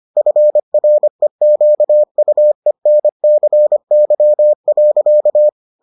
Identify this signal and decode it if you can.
Morse code, heard as a keyed tone, about 25 wpm, 600 Hz. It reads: FREQUENCY.